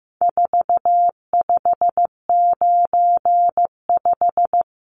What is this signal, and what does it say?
Morse code, heard as a keyed tone, about 15 wpm, 700 Hz. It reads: 4595
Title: Numbers - 4-Digit 15wpm